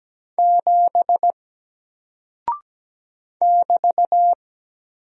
Morse code, heard as a keyed tone, about 17 wpm, 700 Hz. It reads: 7 <BT>
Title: Single Character (Letter:Number:Punctuation:Prosign) 17wpm